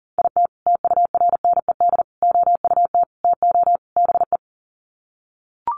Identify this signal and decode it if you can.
Morse code, heard as a keyed tone, about 40 wpm, 700 Hz. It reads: IT TURNED OUT TO BE
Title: Sets of 5 Words 40wpm